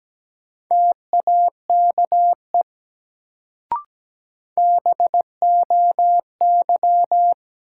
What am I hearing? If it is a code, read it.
Morse code, heard as a keyed tone, about 17 wpm, 700 Hz. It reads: TAKE BOY
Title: Sentences from Top 200 Words - New Word Drill - 17wpm